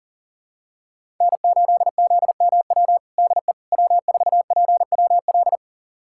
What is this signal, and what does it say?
Morse code, heard as a keyed tone, about 40 wpm, 700 Hz. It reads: N8ZMW DE W4PWL
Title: USA Callsign DE Callsign - 5 Characters - 40wpm